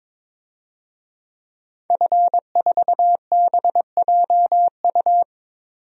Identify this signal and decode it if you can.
Morse code, heard as a keyed tone, about 22 wpm, 700 Hz. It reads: F4BJU
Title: DXCC-100 Call Signs - 5 Characters - 22wpm